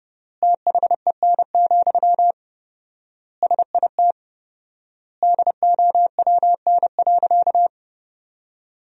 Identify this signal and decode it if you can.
Morse code, heard as a keyed tone, about 30 wpm, 700 Hz. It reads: THEN, SIT DOWN.
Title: 4x Word-Spacing - Sentences from Top 500 Words - 30wpm